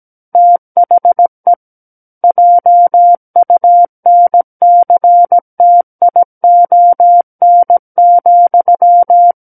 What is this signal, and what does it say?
Morse code, heard as a keyed tone, about 17 wpm, 700 Hz. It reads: THE JUNCTION,
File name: Book One / 08-Chapter 8 - 17 WPM